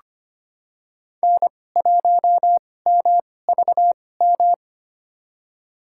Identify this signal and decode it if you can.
Morse code, heard as a keyed tone, about 25 wpm, spaced as for 18 wpm, 700 Hz. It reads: N1MVM
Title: USA Call Signs - 2x Character Spacing - 25wpm